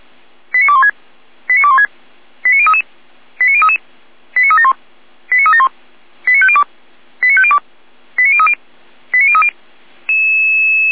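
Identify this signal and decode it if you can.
Five-tone selective call (ZVEI), heard as quick runs of five tones, tones of about 70 ms. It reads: 89216 89216 89033 89033 89371 89371 89502 89502 89030 89030
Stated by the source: ZVEI five-tone selective call 89003 received by SDR.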